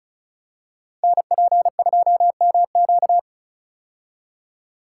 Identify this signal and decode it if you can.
Morse code, heard as a keyed tone, about 34 wpm, 690 Hz. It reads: NP2MQ
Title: Code Groups - 5 Letter:Numbers - 35wpm